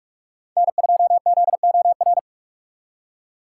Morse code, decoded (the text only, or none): N2ZOR